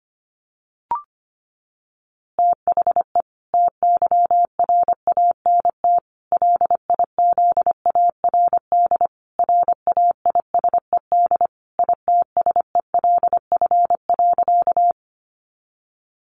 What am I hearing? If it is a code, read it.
Morse code, heard as a keyed tone, about 25 wpm, 700 Hz. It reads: THE TYRANT LIZARD RAISED ITSELF.
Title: A Sound Of Thunder 25wpm